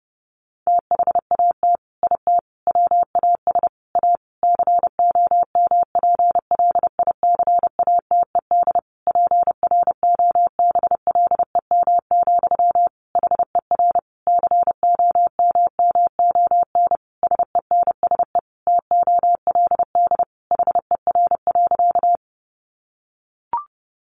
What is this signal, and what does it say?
Morse code, heard as a keyed tone, about 30 wpm, 700 Hz. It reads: THAT IT WAS A COMPLICATED PROBLEM, HER COMMON SENSE TOLD HER.
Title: Sentences from Top 1200 Words 30wpm